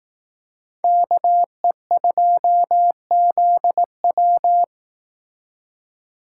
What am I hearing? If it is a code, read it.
Morse code, heard as a keyed tone, about 18 wpm, 700 Hz. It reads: KE2ZW
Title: Speed Racing - USA 2x2 Call Signs 15wpm